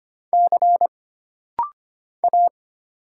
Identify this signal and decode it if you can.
Morse code, heard as a keyed tone, about 25 wpm, 700 Hz. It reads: C A
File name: Single Character (Letter:Number:Punctuation:Prosign) - Rapid-Fire - 25wpm